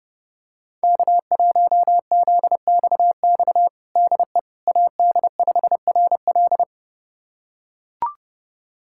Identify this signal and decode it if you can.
Morse code, heard as a keyed tone, about 32 wpm, 700 Hz. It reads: K1ZXX DE AD5RL
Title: USA Callsign DE Callsign - 5 Characters - 30wpm